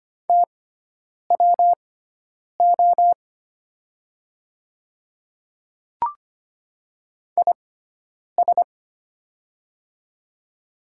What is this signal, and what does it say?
Morse code, heard as a keyed tone, about 25 wpm, spaced as for 9 wpm, 700 Hz. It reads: TWO IS
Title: 6x Character Spacing - Top 100 Words - 25wpm